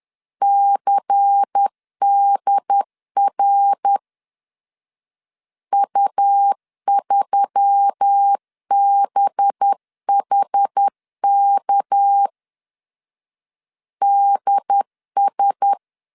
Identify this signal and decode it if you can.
Morse code, heard as a keyed tone, about 10 wpm, spaced as for 7 wpm, 800 Hz. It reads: CDRU3 BHKDS
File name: CW / DET / 1 6 1500C S B 10wpm 4137